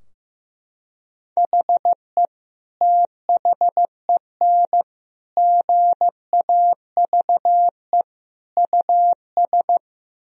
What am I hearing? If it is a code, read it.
Morse code, heard as a keyed tone, about 15 wpm, 700 Hz. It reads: HE THEN GAVE US
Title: Sentences from Top 1000 Words - Review Part 4 - 15wpm